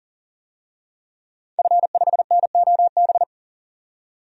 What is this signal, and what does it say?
Morse code, heard as a keyed tone, about 40 wpm, 700 Hz. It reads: F5NOB
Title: DXCC-100 Call Signs - 5 Characters - 40wpm